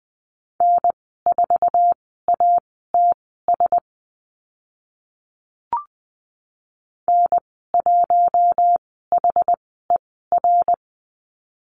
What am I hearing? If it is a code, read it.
Morse code, heard as a keyed tone, about 20 wpm, spaced as for 14 wpm, 700 Hz. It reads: N4ATS N1HER
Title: 069 Lesson 12g - Letter d - Call Signs 20wpm